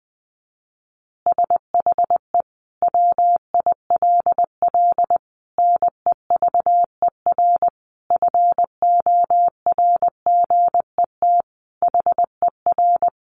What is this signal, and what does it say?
Morse code, heard as a keyed tone, about 20 wpm, 700 Hz. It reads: SHE WILL NEVER FORGET HER
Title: Additional Easy Phrases 20wpm